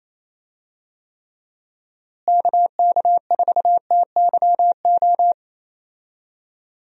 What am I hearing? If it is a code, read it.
Morse code, heard as a keyed tone, about 28 wpm, 700 Hz. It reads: KK4TYO
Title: USA Call Signs - 6 Characters - Repeated 3x - 28wpm